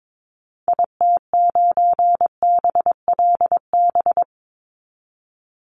Morse code, heard as a keyed tone, about 22 wpm, 700 Hz. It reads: IT9BLB
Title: Call Signs Repeated 3x - 22wpm